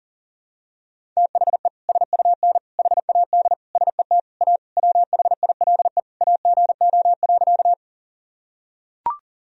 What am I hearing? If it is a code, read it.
Morse code, heard as a keyed tone, about 37 wpm, 700 Hz. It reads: THE SUN HAD SET A WHILE AGO.
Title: Sentences from Top 1000 Words - Review Part 1 - 40wpm